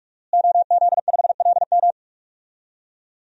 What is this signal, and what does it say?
Morse code, heard as a keyed tone, about 46 wpm, 700 Hz. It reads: OZ5LM
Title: DXCC-100 Call Signs - 5 Characters - Repeated 3x - 45wpm